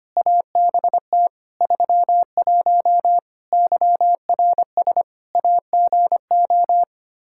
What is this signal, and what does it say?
Morse code, heard as a keyed tone, about 25 wpm, 700 Hz. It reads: ABT 31 YRS AGO
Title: Practice QSO - Exchange 3 - More Info - 25wpm